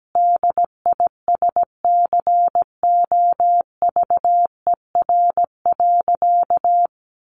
Morse code, heard as a keyed tone, about 17 wpm, 700 Hz. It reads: DISCOVER.
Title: Sentences from Top 1100 Words 17wpm